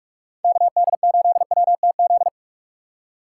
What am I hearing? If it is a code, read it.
Morse code, heard as a keyed tone, about 46 wpm, 690 Hz. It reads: KD8WTZ
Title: USA Call Signs - 6 Characters - 3x Character Spacing - 45wpm